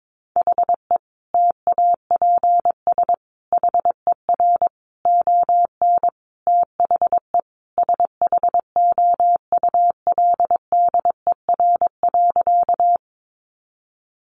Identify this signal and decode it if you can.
Morse code, heard as a keyed tone, about 22 wpm, 700 Hz. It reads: HE TAPS HER ON THE SHOULDER.
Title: Additional Easy Phrases 22wpm